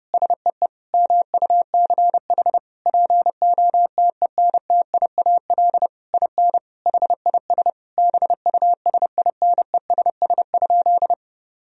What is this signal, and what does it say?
Morse code, heard as a keyed tone, about 32 wpm, 700 Hz. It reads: SEE MUCH POTENTIAL IN HIS BUSINESS?
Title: Sentences from Top 1000 Words - Review Part 4 - 30wpm